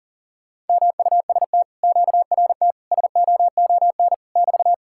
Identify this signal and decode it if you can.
Morse code, heard as a keyed tone, about 48 wpm, 700 Hz. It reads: MUST QRT SOON <BT>
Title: Speed-Racing Common QSO Phrases - 40wpm